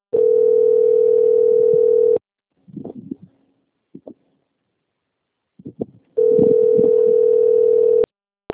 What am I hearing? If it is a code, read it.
Ringback tone (North America), 440 + 480 Hz.